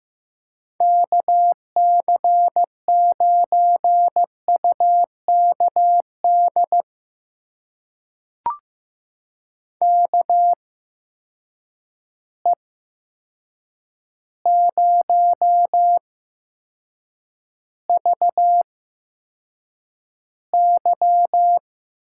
Morse code, heard as a keyed tone, about 15 wpm, spaced as for 7 wpm, 700 Hz. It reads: KC9UKD KE0VY
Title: USA Call Signs - 6 Characters - 8x Character Spacing - 15wpm